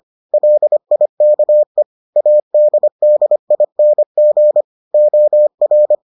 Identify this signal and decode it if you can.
Morse code, heard as a keyed tone, about 25 wpm, 600 Hz. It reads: LIKE ADDING OR